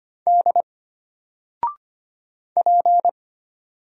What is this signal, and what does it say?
Morse code, heard as a keyed tone, about 25 wpm, 700 Hz. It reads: D P
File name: Single-Letter-Number-25wpm